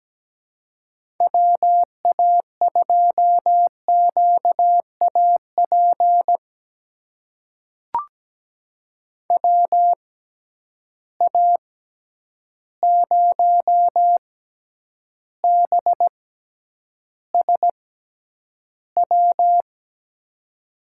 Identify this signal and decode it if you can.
Morse code, heard as a keyed tone, about 17 wpm, spaced as for 9 wpm, 700 Hz. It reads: WA2QAP WA0BSW
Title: USA Call Signs - 6 Characters - 6x Character Spacing - 17wpm